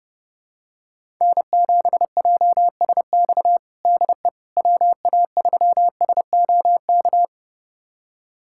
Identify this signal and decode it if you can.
Morse code, heard as a keyed tone, about 30 wpm, 700 Hz. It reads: N7JSX DE WA3SOK
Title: USA Callsign DE Callsign 30wpm